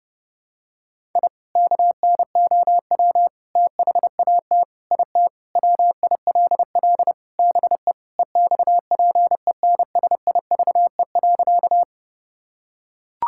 Morse code, heard as a keyed tone, about 30 wpm, 700 Hz. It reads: I KNOW THAT IT WILL BE EXPENSIVE.
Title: Sentences from Top 1000 Words - Review Part 6 - 30wpm